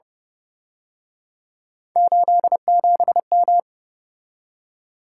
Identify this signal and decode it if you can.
Morse code, heard as a keyed tone, about 31 wpm, 700 Hz. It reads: 87M